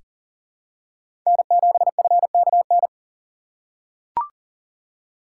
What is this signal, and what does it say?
Morse code, heard as a keyed tone, about 40 wpm, 700 Hz. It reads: N7FKN
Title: USA Call Signs - 5 Characters - 6x Character Spacing - 40wpm